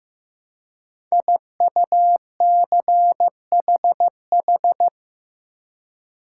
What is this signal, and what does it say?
Morse code, heard as a keyed tone, about 15 wpm, 700 Hz. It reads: IUCHH
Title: Code Groups - 5 Letters - 15wpm